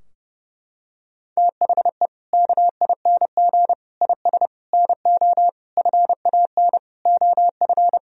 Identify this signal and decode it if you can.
Morse code, heard as a keyed tone, about 32 wpm, 700 Hz. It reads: THE KING IS NO FAN OF
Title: Sentences from Top 1000 Words - Review Part 4 - 30wpm